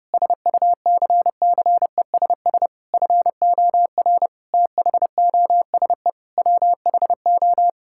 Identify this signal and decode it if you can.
Morse code, heard as a keyed tone, about 32 wpm, 700 Hz. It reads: SUCCESS FOR THOSE WHO
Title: Sentences from Top 1000 Words - Review Part 6 - 30wpm